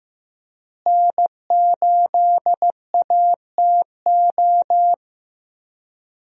Morse code, heard as a keyed tone, about 15 wpm, 700 Hz. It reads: N8ATO
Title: USA Call Signs - 5 Characters - Repeated 3x - 15wpm